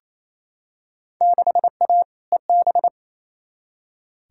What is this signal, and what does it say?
Morse code, heard as a keyed tone, about 28 wpm, 700 Hz. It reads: 6A EB
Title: Field Day Exchanges Encore - 28wpm